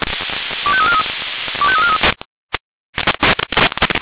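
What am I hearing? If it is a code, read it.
Five-tone selective call (ZVEI), heard as quick runs of five tones, tones of about 70 ms. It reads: 25343 25343
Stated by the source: ZVEI five-tone selective call received on an SDR.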